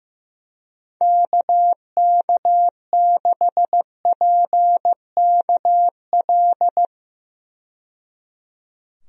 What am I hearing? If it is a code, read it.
Morse code, heard as a keyed tone, about 15 wpm, 700 Hz. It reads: KK6PKL
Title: USA Call Signs - 6 Characters - 15wpm